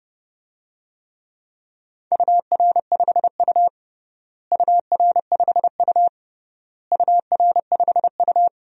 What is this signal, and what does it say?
Morse code, heard as a keyed tone, about 30 wpm, 700 Hz. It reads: UR5U UR5U UR5U
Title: DXCC-100 Call Signs - 4 Characters - Repeated 3x - 30wpm